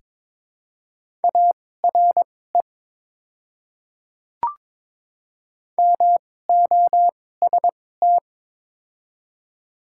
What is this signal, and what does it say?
Morse code, heard as a keyed tone, about 22 wpm, spaced as for 16 wpm, 700 Hz. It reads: ARE MOST
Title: 2x Character Spacing - Top 100 Words - 22wpm